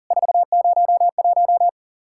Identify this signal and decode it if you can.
Morse code, heard as a keyed tone, about 40 wpm, 700 Hz. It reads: 401
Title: Worked All Europe DX Contest - QTC Exchange - 40wpm